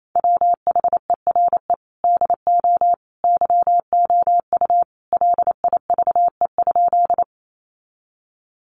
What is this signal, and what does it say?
Morse code, heard as a keyed tone, about 28 wpm, 700 Hz. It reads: WHERE DO YOU LIVE?
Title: Sentences from Top 200 Words 28wpm